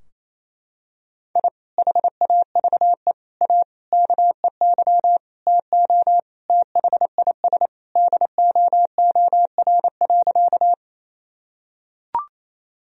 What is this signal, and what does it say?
Morse code, heard as a keyed tone, about 28 wpm, 700 Hz. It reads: I HAVE A KEY TO THIS DOOR.
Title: Sentences from Top 1000 Words - Review Part 4 - 28wpm